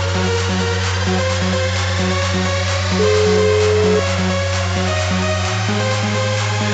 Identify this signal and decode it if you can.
Ringback tone (Europe, 425 Hz), with noise.